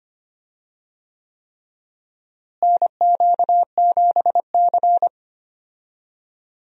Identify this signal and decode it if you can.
Morse code, heard as a keyed tone, about 25 wpm, 700 Hz. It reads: NQ7C
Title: USA Call Signs - 4 Characters - Repeated 3x - 25wpm